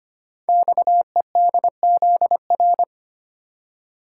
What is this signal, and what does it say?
Morse code, heard as a keyed tone, about 25 wpm, 700 Hz. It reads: XEDZR
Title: Code Groups - 5 Letters - 25wpm